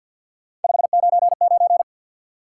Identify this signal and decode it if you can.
Morse code, heard as a keyed tone, about 50 wpm, 700 Hz. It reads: 599
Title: Top 90 QSO Elements 50wpm